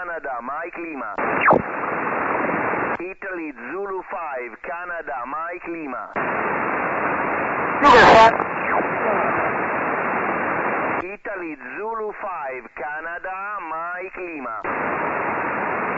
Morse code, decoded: none